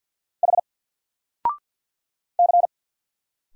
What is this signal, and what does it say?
Morse code, heard as a keyed tone, about 50 wpm, 700 Hz. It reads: H C
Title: Single Letter-Number 50wpm